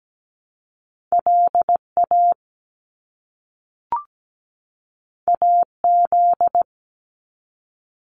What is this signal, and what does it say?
Morse code, heard as a keyed tone, about 17 wpm, 700 Hz. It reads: LA AZ
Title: US States 17wpm